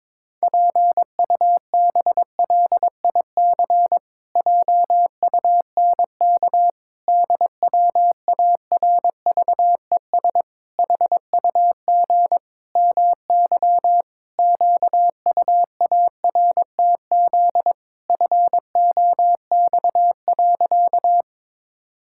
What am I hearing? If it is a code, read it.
Morse code, heard as a keyed tone, about 22 wpm, 700 Hz. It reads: PUBLIC JUNK DWARVES HUG MY QUARTZ FOX.
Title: Pangram Challenge v2 - 22wpm